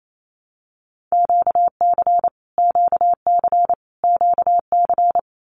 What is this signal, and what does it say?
Morse code, heard as a keyed tone, about 28 wpm, 700 Hz. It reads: QC QC QC
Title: Canadian Provinces and Territories - Repeated 3x - 28wpm